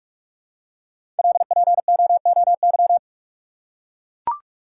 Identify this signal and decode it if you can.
Morse code, heard as a keyed tone, about 44 wpm, 700 Hz. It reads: LPOOY
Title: Code Groups - 5 Letters - 45wpm